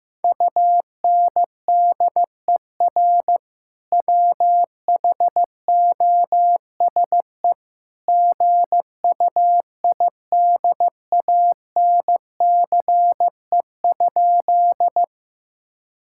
Morse code, heard as a keyed tone, about 14 wpm, 700 Hz. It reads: UNDER WHOSE GUIDANCE?
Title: Sentences from Top 1600 Words 15wpm